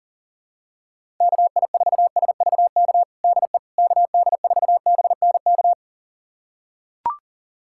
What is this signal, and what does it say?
Morse code, heard as a keyed tone, about 40 wpm, 700 Hz. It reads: KI4SVK DE KD4BNK
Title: USA Callsign DE Callsign - 6 Characters - 40wpm